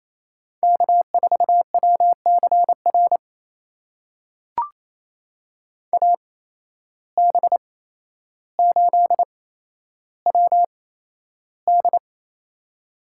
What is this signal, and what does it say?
Morse code, heard as a keyed tone, about 28 wpm, spaced as for 12 wpm, 700 Hz. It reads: K4WCR AB8WD
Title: USA Call Signs - 8x Character Spacing - 28wpm